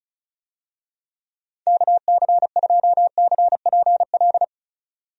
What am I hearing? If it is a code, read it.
Morse code, heard as a keyed tone, about 36 wpm, 700 Hz. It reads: KC2CPL
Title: USA Call Signs - 6 Characters - 35wpm